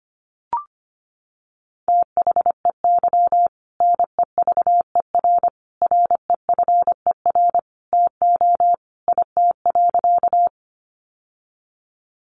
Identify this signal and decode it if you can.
Morse code, heard as a keyed tone, about 25 wpm, 700 Hz. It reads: THEY NEVER REFER TO IT.
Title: Sentences from Top 1000 Words 25wpm